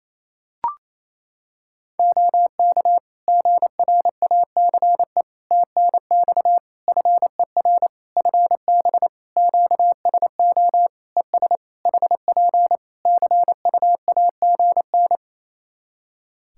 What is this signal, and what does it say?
Morse code, heard as a keyed tone, about 28 wpm, 700 Hz. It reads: OK GRACE TNX FER FB QSO ES HP CUAGN
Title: Practice QSO - Exchange 4 - Ending - 28wpm